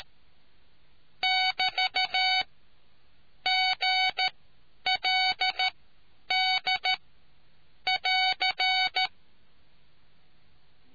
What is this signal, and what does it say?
Morse code, heard as a keyed tone, about 13 wpm, spaced as for 9 wpm, 750 Hz. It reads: <BT> GLD <AR>